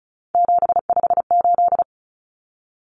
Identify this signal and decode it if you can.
Morse code, heard as a keyed tone, about 34 wpm, 700 Hz. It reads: 758
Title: Numbers - 3-Digit 35wpm